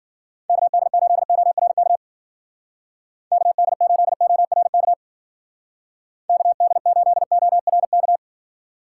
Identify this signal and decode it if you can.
Morse code, heard as a keyed tone, about 52 wpm, 700 Hz. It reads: KD8ORK KD8ORK KD8ORK